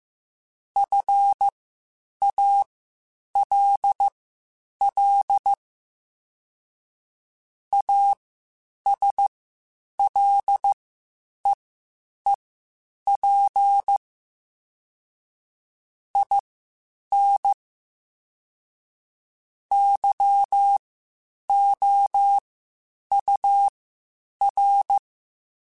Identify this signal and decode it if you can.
Morse code, heard as a keyed tone, about 15 wpm, spaced as for 8 wpm, 800 Hz. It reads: FALL ASLEEP IN YOUR
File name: Quotes Spaced / Quotes Spaced-mp3s / ChildAsleep sp @13